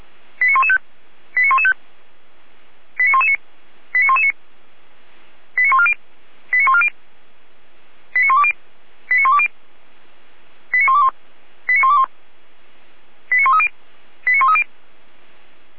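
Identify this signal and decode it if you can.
Five-tone selective call (ZVEI), heard as quick runs of five tones, tones of about 70 ms. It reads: 89115 89115 89119 89119 89140 89140 89120 89120 89121 89121 89130 89130
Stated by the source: ZVEI five-tone selective call received on an SDR.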